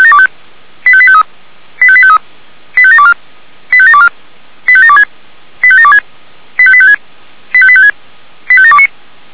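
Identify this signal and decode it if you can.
Five-tone selective call (ZVEI), heard as quick runs of five tones, tones of about 70 ms. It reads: 85853 85853 85724 85724 85726 85726 85856 85856 85729
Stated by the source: ZVEI five-tone selective call received by SDR.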